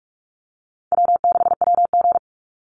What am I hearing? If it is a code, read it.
Morse code, heard as a keyed tone, about 44 wpm, 700 Hz. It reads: W6WG
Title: Easy Call Signs Repeated 3x - 45wpm